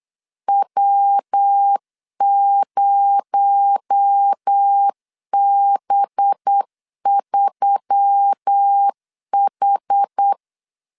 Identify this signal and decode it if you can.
Morse code, heard as a keyed tone, about 8 wpm, 800 Hz. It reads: W0B3H
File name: LEZIONI / 6 PKX0 / 6 PKX0 8wpm 3341